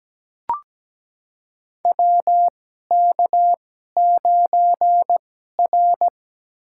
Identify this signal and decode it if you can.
Morse code, heard as a keyed tone, about 17 wpm, spaced as for 12 wpm, 700 Hz. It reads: WK9R
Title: USA Call Signs - 4 Characters - 2x Character Spacing - 17wpm